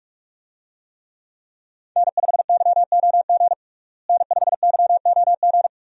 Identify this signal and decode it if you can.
Morse code, heard as a keyed tone, about 46 wpm, 700 Hz. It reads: N5YOG N5YOG
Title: USA Call Signs - 5 Characters - Repeated 2x - 45wpm